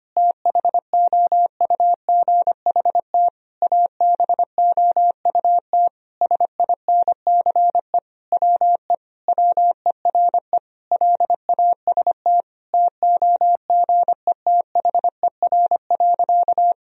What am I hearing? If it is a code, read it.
Morse code, heard as a keyed tone, about 25 wpm, 700 Hz. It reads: THOUGHT ABOUT SINCE WE WERE LAST TOGETHER.
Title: Sentences from Top 1000 Words - Review Part 1 - 25wpm